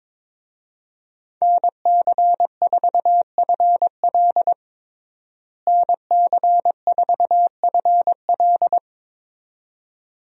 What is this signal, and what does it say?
Morse code, heard as a keyed tone, about 22 wpm, 700 Hz. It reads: NC4FL NC4FL
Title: USA Call Signs - 5 Characters - Repeated 2x - 22wpm